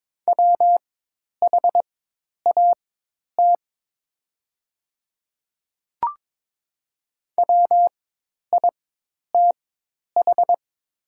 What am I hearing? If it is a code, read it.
Morse code, heard as a keyed tone, about 22 wpm, spaced as for 10 wpm, 700 Hz. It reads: WHAT WITH
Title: 4x Character Spacing - Top 100 Words - 22wpm